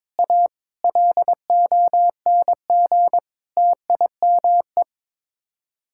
Morse code, heard as a keyed tone, about 22 wpm, 700 Hz. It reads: A LONG TIME
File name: practice / 22wpm / Sets of 7 Words - 22wpm